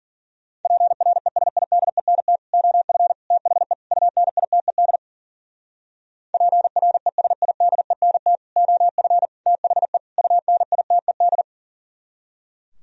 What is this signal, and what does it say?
Morse code, heard as a keyed tone, about 51 wpm, 700 Hz. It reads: PRESIDENT OF THE UNITED PRESIDENT OF THE UNITED